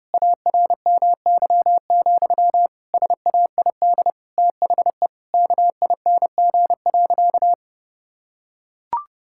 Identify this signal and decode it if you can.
Morse code, heard as a keyed tone, about 32 wpm, 700 Hz. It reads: ARMY, SAID THE KING.
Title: Sentences from Top 1000 Words - Review Part 5 - 30wpm